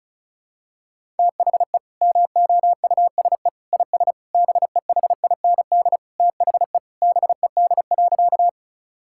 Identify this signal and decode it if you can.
Morse code, heard as a keyed tone, about 34 wpm, 700 Hz. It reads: THE MOUSE IS BEHIND THE BED.
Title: Sentences from Top 1000 Words - Review Part 5 - 35wpm